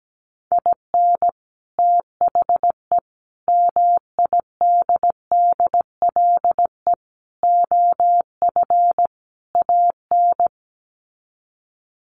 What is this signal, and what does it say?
Morse code, heard as a keyed tone, about 17 wpm, 700 Hz. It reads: IN THE MIDDLE OF AN
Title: Sets of 5 Words Encore2 17wpm